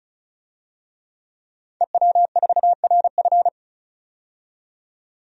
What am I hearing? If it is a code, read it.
Morse code, heard as a keyed tone, about 36 wpm, 700 Hz. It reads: EW4RF
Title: DXCC-100 Call Signs - 5 Characters - 35wpm